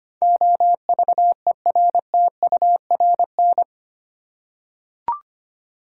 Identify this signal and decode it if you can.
Morse code, heard as a keyed tone, about 25 wpm, 700 Hz. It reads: OVERTURN
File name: Eight Letter Words - 25wpm